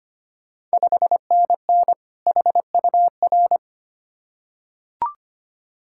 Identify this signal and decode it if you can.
Morse code, heard as a keyed tone, about 25 wpm, 700 Hz. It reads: 5NN HUR
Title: Ontario QSO Party Exchange 25wpm